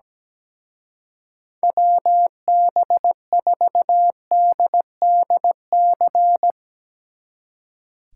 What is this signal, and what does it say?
Morse code, heard as a keyed tone, about 17 wpm, 700 Hz. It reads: WB4DDC
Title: Speed Racing - USA 2x3 Call Signs 17wpm